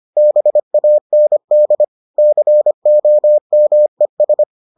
Morse code, heard as a keyed tone, about 25 wpm, 600 Hz. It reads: BAND COMES